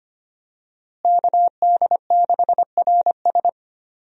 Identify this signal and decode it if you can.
Morse code, heard as a keyed tone, about 25 wpm, 700 Hz. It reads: KD6RS
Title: USA Call Signs - 5 Characters - Repeated 3x - 25wpm